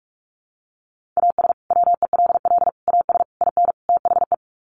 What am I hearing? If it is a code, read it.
Morse code, heard as a keyed tone, about 44 wpm, 700 Hz. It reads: AS WELL AS IN THE
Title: Sets of 5 Words Encore 45wpm